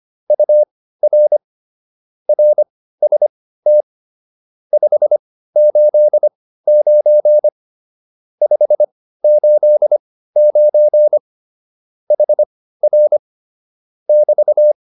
Morse code, heard as a keyed tone, about 25 wpm, spaced as for 15 wpm, 600 Hz. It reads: UR RST 589 589 HR <BT>